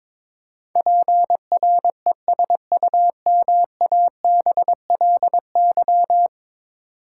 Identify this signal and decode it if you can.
Morse code, heard as a keyed tone, about 22 wpm, 700 Hz. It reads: PRESUMABLY
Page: morse-code-ninja-practice-22wpm